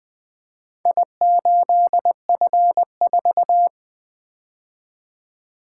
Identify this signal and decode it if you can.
Morse code, heard as a keyed tone, about 20 wpm, 700 Hz. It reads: I8F4
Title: Four Letter-Number Combinations - 20wpm